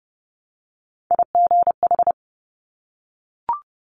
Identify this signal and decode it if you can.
Morse code, heard as a keyed tone, about 30 wpm, 700 Hz. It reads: IGH
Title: Top Three Letter Combinations in Words 30wpm